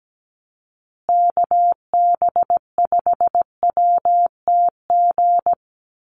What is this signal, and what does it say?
Morse code, heard as a keyed tone, about 17 wpm, 700 Hz. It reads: KB5WTG
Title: USA Call Signs Repeated 3x 17wpm